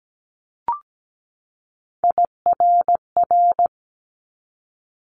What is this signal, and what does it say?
Morse code, heard as a keyed tone, about 17 wpm, 700 Hz. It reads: IRR
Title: Multi-letter Phonemes 17wpm